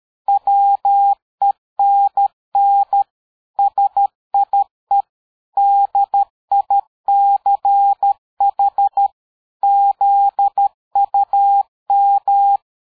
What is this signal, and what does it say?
Morse code, heard as a keyed tone, about 13 wpm, 800 Hz. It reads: WENN SIE DICH ZUM
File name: dk5ke / PersWorte 60